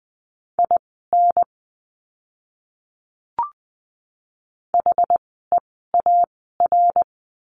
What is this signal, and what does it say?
Morse code, heard as a keyed tone, about 20 wpm, spaced as for 14 wpm, 700 Hz. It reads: IN HEAR
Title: 075 Lesson 13f - Letter L - Words 20wpm